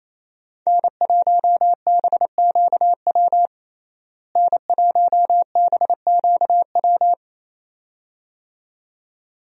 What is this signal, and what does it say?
Morse code, heard as a keyed tone, about 27 wpm, 700 Hz. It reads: N1BQW N1BQW
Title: USA Call Signs - 5 Characters - Repeated 2x - 28wpm